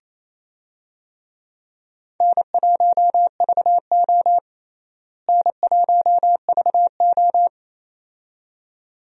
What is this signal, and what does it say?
Morse code, heard as a keyed tone, about 28 wpm, 700 Hz. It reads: N1VO N1VO
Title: USA Call Signs - 4 Characters - Repeated 2x - 28wpm